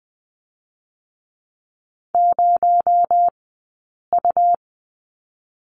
Morse code, heard as a keyed tone, about 20 wpm, 700 Hz. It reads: 0 U
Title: 217 Lesson 35d - Number 0 - Two Characters 20wpm